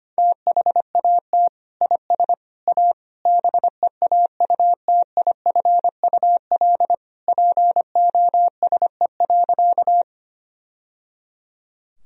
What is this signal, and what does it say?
Morse code, heard as a keyed tone, about 25 wpm, 700 Hz. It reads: THAT IS A BEAUTIFUL POSE.
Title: Sentences from Top 1000 Words - Review Part 2 - 25wpm